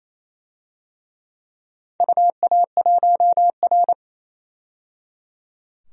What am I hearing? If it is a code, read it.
Morse code, heard as a keyed tone, about 27 wpm, 700 Hz. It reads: UA1R
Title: DXCC-100 Call Signs - 4 Characters - 28wpm